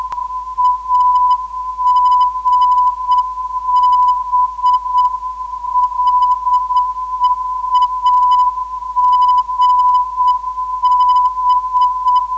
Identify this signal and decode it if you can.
Morse code, heard as a keyed tone, about 8 wpm, 1000 Hz. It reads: AGB LE AGB
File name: ndb zvok / AGB-LE 318kHz